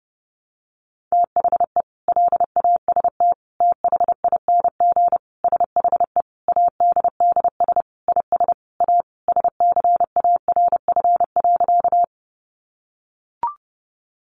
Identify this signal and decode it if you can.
Morse code, heard as a keyed tone, about 30 wpm, 700 Hz. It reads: THE LAST THING SHE ADDS IS A SCARF.
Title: Easy Phrases 30wpm